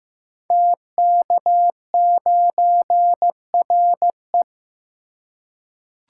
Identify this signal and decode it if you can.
Morse code, heard as a keyed tone, about 15 wpm, 700 Hz. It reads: TK9RE
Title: Code Groups - 5 Letter:Numbers - 15wpm